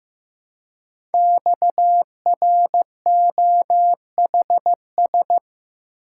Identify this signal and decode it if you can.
Morse code, heard as a keyed tone, about 15 wpm, 700 Hz. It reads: XROHS
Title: Code Groups - 5 Letters - 15wpm